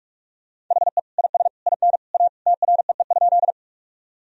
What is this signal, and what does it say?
Morse code, heard as a keyed tone, about 46 wpm, 700 Hz. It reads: HE IS IN A TREE?
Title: Sentences from Top 1000 Words - Review Part 1 - 45wpm